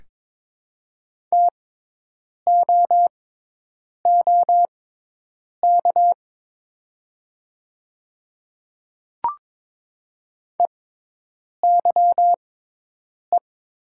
Morse code, heard as a keyed tone, about 22 wpm, spaced as for 8 wpm, 700 Hz. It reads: TOOK EYE
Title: 6x Character Spacing - Top 300 Words - 22wpm